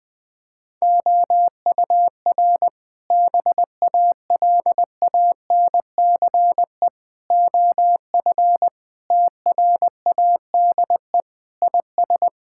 Morse code, heard as a keyed tone, about 21 wpm, 700 Hz. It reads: OUR BALANCE OF TRADE IS
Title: Sentences from Top 1000 Words - Review Part 6 - 20wpm